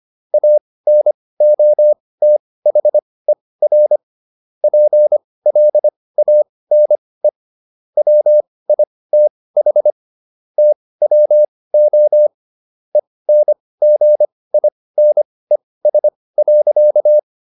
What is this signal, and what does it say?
Morse code, heard as a keyed tone, about 25 wpm, spaced as for 18 wpm, 600 Hz. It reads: ANOTHER PLANE WITH TWO ENGINES.